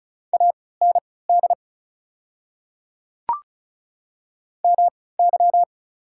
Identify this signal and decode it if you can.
Morse code, heard as a keyed tone, about 36 wpm, spaced as for 20 wpm, 690 Hz. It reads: AND MY
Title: 3x Character Spacing - Top 100 Words - 35wpm